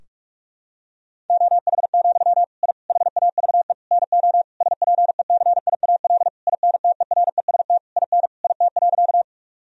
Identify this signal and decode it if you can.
Morse code, heard as a keyed tone, about 41 wpm, 700 Hz. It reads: OH, I HAVE NO SPECIAL INTEREST IN IT.
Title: Sentences from Top 1000 Words - Review Part 3 - 45wpm